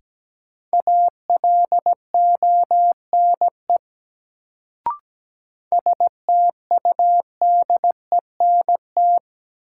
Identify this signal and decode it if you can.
Morse code, heard as a keyed tone, about 17 wpm, 700 Hz. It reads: ALONE STUDENT
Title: Sentences from Top 1000 Words - New Word Drill - 17wpm